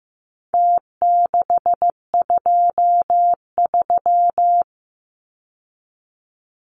Morse code, heard as a keyed tone, about 15 wpm, 700 Hz. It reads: T623
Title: Numbers - Contest Style - 4-Digit 15wpm